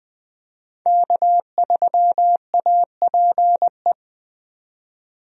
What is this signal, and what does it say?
Morse code, heard as a keyed tone, about 21 wpm, 700 Hz. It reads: K3APE
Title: USA Call Signs - 5 Characters - 20wpm